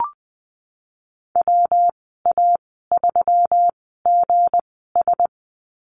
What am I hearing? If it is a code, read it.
Morse code, heard as a keyed tone, about 20 wpm, spaced as for 14 wpm, 700 Hz. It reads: WA3GS
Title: 193 Lesson 31g - Letter v - Call Signs 20wpm